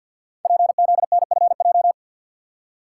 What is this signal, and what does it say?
Morse code, heard as a keyed tone, about 53 wpm, 700 Hz. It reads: PZNFJ
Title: Code Groups - 5 Letters - 50wpm